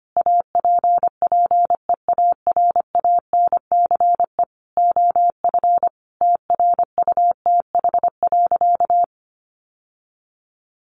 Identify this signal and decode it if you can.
Morse code, heard as a keyed tone, about 25 wpm, 700 Hz. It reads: APPEARANCE OF TRUTH.
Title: Sentences from Top 1100 Words 25wpm